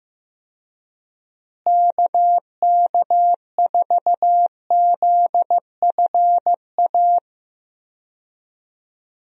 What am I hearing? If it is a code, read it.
Morse code, heard as a keyed tone, about 15 wpm, 700 Hz. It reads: KK4ZFA
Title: USA Call Signs - 6 Characters - Repeated 3x - 15wpm